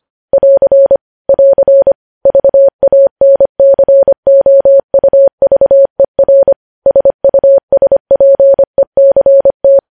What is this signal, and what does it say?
Morse code, heard as a keyed tone, about 25 wpm, 550 Hz. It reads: <AR> <AR> VANCOUVER SUSPECT
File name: podcast / News headlines-1x-25x25